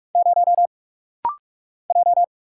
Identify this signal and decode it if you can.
Morse code, heard as a keyed tone, about 44 wpm, 700 Hz. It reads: 0 J
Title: Single Letter-Number Alt - Rapid-Fire 45wpm